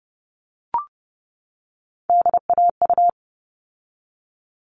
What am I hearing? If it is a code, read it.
Morse code, heard as a keyed tone, about 30 wpm, 700 Hz. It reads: DAU
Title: All Cut Numbers - 3-Digit 30wpm